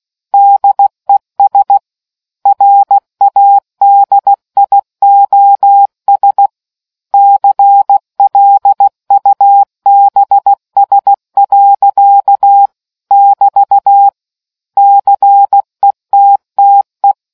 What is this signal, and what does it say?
Morse code, heard as a keyed tone, about 16 wpm, 800 Hz. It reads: DES RADIOS CLUBS. <BT> CETTE